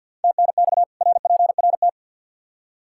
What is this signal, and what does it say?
Morse code, heard as a keyed tone, about 50 wpm, 700 Hz. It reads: TNX RPRT
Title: Practice QSO - Exchange 2 - Info - 50wpm